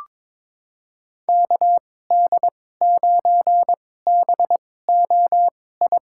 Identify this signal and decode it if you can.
Morse code, heard as a keyed tone, about 22 wpm, spaced as for 16 wpm, 700 Hz. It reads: KD9BOI
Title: USA Call Signs - 6 Characters - 2x Character Spacing - 22wpm